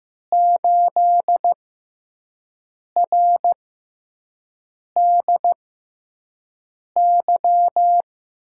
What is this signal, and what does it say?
Morse code, heard as a keyed tone, about 15 wpm, spaced as for 5 wpm, 700 Hz. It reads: 8RDY
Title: USA Call Signs - 6 Characters - 6x Character Spacing - 15wpm